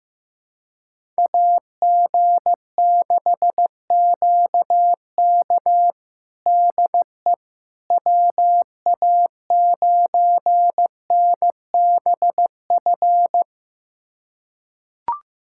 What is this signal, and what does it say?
Morse code, heard as a keyed tone, about 15 wpm, 700 Hz. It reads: AG6QK DE WA9NBF
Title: USA Callsign DE Callsign 15wpm